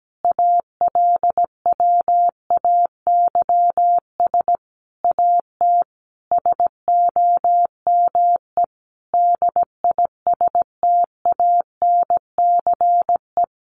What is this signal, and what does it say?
Morse code, heard as a keyed tone, about 17 wpm, 700 Hz. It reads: ALWAYS AT SOME DISTANCE
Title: Sentences from Top 1200 Words 17wpm